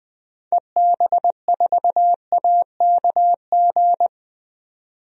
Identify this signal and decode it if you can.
Morse code, heard as a keyed tone, about 20 wpm, 700 Hz. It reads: EB4AKG
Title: DXCC-100 Call Signs - 6 Characters - 20wpm